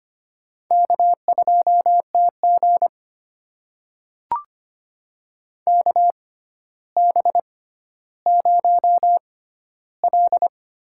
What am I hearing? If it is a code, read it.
Morse code, heard as a keyed tone, about 25 wpm, spaced as for 13 wpm, 700 Hz. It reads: K2TG KB0L
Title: USA Call Signs - 4 Characters - 6x Character Spacing - 25wpm